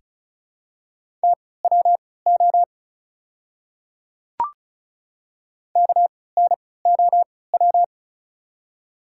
Morse code, heard as a keyed tone, about 34 wpm, spaced as for 20 wpm, 700 Hz. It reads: TWO KNOW
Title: 3x Character Spacing - Top 300 Words - 35wpm